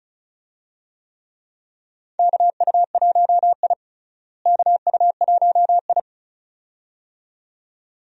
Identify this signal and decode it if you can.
Morse code, heard as a keyed tone, about 36 wpm, 700 Hz. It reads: KU1I KU1I
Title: USA Call Signs - 4 Characters - Repeated 2x - 35wpm